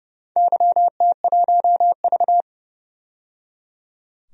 Morse code, heard as a keyed tone, about 30 wpm, 700 Hz. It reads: YT1V
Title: DXCC-100 Call Signs - 4 Characters - 30wpm